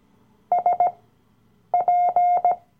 Morse code, heard as a keyed tone, about 18 wpm, spaced as for 8 wpm, 710 Hz. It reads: SP
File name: Workshop Dateien / CW-Online-Kurs / Lektion 007